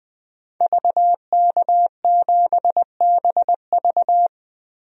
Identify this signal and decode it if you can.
Morse code, heard as a keyed tone, about 21 wpm, 700 Hz. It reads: VK7BV
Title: DXCC-100 Call Signs - 5 Characters - Repeated 2x - 20wpm